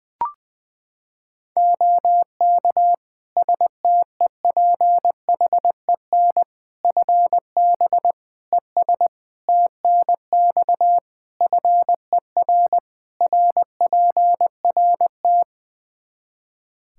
Morse code, heard as a keyed tone, about 21 wpm, 700 Hz. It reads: OK STEPHEN FB ES TNX FER RPRT
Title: Practice QSO - Exchange 2 - Info - 20wpm